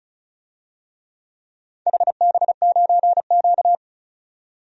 Morse code, heard as a keyed tone, about 36 wpm, 690 Hz. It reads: HB9Q
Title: DXCC-100 Call Signs - 4 Characters - 35wpm